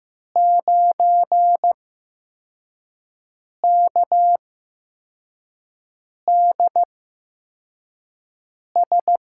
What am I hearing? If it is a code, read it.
Morse code, heard as a keyed tone, about 15 wpm, spaced as for 4 wpm, 700 Hz. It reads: 9KDS
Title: USA Call Signs - 6 Characters - 8x Character Spacing - 15wpm